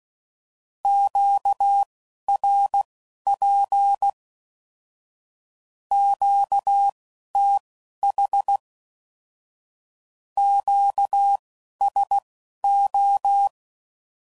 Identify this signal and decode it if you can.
Morse code, heard as a keyed tone, about 16 wpm, spaced as for 11 wpm, 800 Hz. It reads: QRP QTH QSO
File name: CW Code Groups @10WPM / Q Calls / Q Calls Once